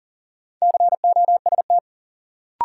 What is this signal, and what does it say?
Morse code, heard as a keyed tone, about 40 wpm, 700 Hz. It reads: COST
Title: Sets of 3 Words - New Word Drill - 40wpm